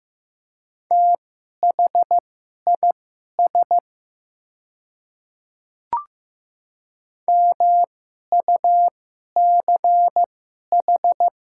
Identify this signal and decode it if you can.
Morse code, heard as a keyed tone, about 15 wpm, spaced as for 11 wpm, 690 Hz. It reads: THIS MUCH
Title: 2x Character Spacing - Top 200 Words - 15wpm